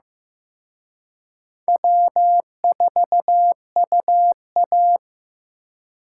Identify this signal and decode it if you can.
Morse code, heard as a keyed tone, about 18 wpm, 700 Hz. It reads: W4UA
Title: Speed Racing - USA 1x2 Call Signs 15wpm